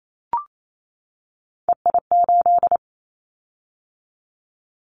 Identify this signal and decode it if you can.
Morse code, heard as a keyed tone, about 28 wpm, 700 Hz. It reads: EI8
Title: Callsign Prefixes for Top 50 DXCC - 28wpm